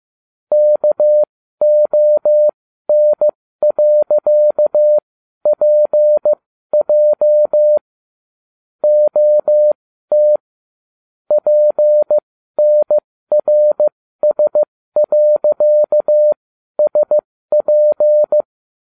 Morse code, heard as a keyed tone, about 15 wpm, spaced as for 12 wpm, 600 Hz. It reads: KON.PJ OT PNRS.SP